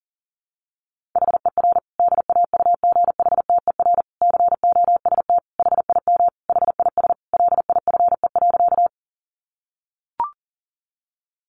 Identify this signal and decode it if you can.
Morse code, heard as a keyed tone, about 40 wpm, 700 Hz. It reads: HER DAUGHTER COST HIM HIS LIFE.
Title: Sentences from Top 1500 Words 40wpm